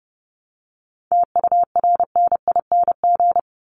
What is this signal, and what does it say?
Morse code, heard as a keyed tone, about 30 wpm, 700 Hz. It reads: TURNING
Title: ING Suffix - Top 100 Words with Base Words 30wpm